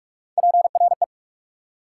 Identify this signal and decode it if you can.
Morse code, heard as a keyed tone, about 42 wpm, 700 Hz. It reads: PRE
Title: Prefixes and Suffixes 45wpm